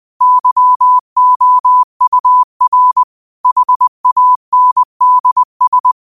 Morse code, heard as a keyed tone, about 20 wpm, 1000 Hz. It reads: YOUR HANDS
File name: qotdfiles / 20wpm